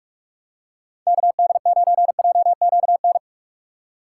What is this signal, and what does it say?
Morse code, heard as a keyed tone, about 46 wpm, 700 Hz. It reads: KD9JQN
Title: USA Call Signs - 6 Characters - 6x Character Spacing - 45wpm